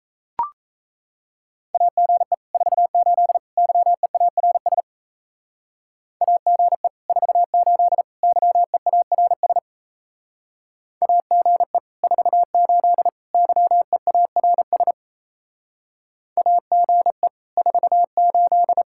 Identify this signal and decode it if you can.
Morse code, heard as a keyed tone, about 37 wpm, 700 Hz. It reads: AGE 48 YEARS AGE 48 YEARS AGE 48 YEARS AGE 48